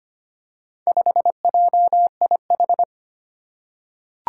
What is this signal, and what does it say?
Morse code, heard as a keyed tone, about 25 wpm, 700 Hz. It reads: 5JIH
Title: Four Challenging Letter-Number Combinations 25wpm